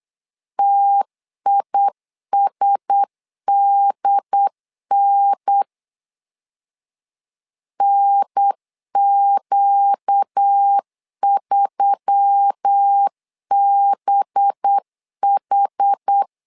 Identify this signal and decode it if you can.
Morse code, heard as a keyed tone, about 8 wpm, 800 Hz. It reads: TISDN NQ3BH
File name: LEZIONI / 5 LBVC / 5 LBV 8wpm 2837